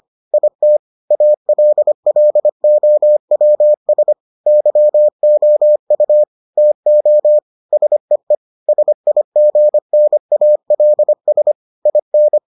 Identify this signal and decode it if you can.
Morse code, heard as a keyed tone, about 25 wpm, 600 Hz. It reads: IT ALLOWS YOU TO SEE SIGNALS IN